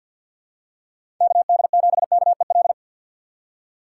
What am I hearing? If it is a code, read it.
Morse code, heard as a keyed tone, about 50 wpm, 700 Hz. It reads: KD7KEL
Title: USA Call Signs - 4x Character Spacing - 50wpm